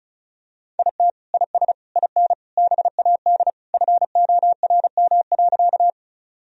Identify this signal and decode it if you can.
Morse code, heard as a keyed tone, about 36 wpm, 700 Hz. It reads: IT IS IN BAD FORM.
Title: Sentences from Top 1000 Words - Review Part 1 - 35wpm